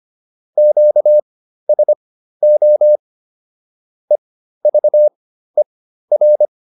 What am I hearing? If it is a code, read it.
Morse code, heard as a keyed tone, about 25 wpm, spaced as for 13 wpm, 600 Hz. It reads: QSO EVER